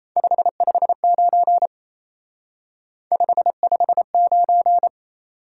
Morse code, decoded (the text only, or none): 559 559